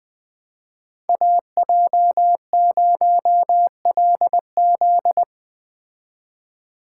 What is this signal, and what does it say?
Morse code, heard as a keyed tone, about 19 wpm, 700 Hz. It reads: AJ0LZ
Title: Speed Racing - USA 2x2 Call Signs 17wpm